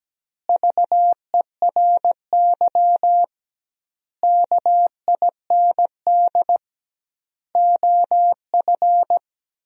Morse code, heard as a keyed tone, about 17 wpm, 700 Hz. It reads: VERY KIND OF
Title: 2x Word-Spacing - Sentences from Top 300 Words - 17wpm